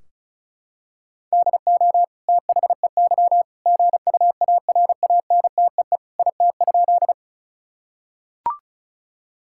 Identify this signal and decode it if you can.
Morse code, heard as a keyed tone, about 36 wpm, 700 Hz. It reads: DO THEY GUARANTEE IT?
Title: Sentences from Top 1600 Words 35wpm